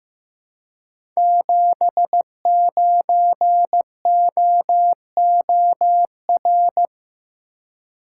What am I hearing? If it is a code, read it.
Morse code, heard as a keyed tone, about 15 wpm, 700 Hz. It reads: 79OOR
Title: Code Groups - 5 Letter:Numbers - 15wpm